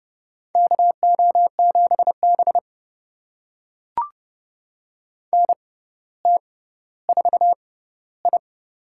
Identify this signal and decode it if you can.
Morse code, heard as a keyed tone, about 30 wpm, spaced as for 15 wpm, 700 Hz. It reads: KO7B NT4I
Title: USA Call Signs - 4 Characters - 6x Character Spacing - 30wpm